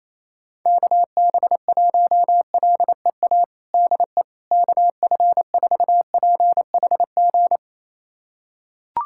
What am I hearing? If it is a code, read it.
Morse code, heard as a keyed tone, about 28 wpm, 700 Hz. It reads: KB1LEA DE KF4PHG
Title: USA Callsign DE Callsign 28wpm